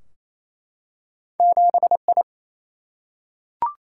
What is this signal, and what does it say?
Morse code, heard as a keyed tone, about 28 wpm, 700 Hz. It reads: 7I
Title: Single-Letter Single-Number 28wpm